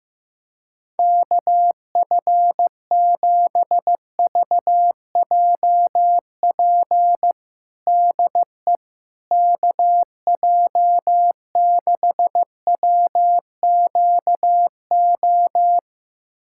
Morse code, heard as a keyed tone, about 15 wpm, 700 Hz. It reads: KF7VJP DE KJ6WQO
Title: USA Callsign DE Callsign 15wpm